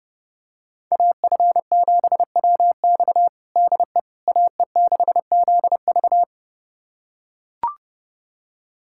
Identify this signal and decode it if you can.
Morse code, heard as a keyed tone, about 32 wpm, 700 Hz. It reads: AF7WX DE AE6ZV
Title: USA Callsign DE Callsign - 5 Characters - 30wpm